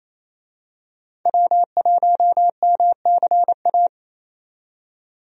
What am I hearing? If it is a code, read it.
Morse code, heard as a keyed tone, about 28 wpm, 700 Hz. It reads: W1MCA
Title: USA Call Signs - 5 Characters - 28wpm